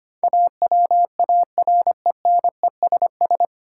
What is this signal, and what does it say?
Morse code, heard as a keyed tone, about 25 wpm, 700 Hz. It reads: AWARENESS
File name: Nine Letter Words - 25wpm